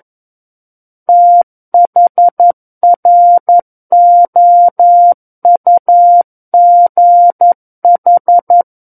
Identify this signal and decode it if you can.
Morse code, heard as a keyed tone, about 11 wpm, 700 Hz. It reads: THROUGH
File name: 11 WPM